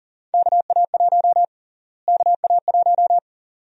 Morse code, heard as a keyed tone, about 40 wpm, 700 Hz. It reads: KA1 KA1
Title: USA Call Sign Prefixes - Repeated 2x - 40wpm